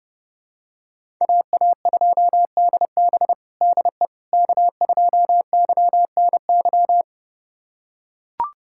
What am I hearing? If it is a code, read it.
Morse code, heard as a keyed tone, about 30 wpm, 700 Hz. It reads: AA2DB DE K2YNY
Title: USA Callsign DE Callsign - 5 Characters - 30wpm